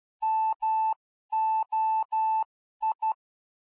Morse code, heard as a keyed tone, about 12 wpm, spaced as for 11 wpm, 880 Hz. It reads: MOI
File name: ardf.quickstart / moi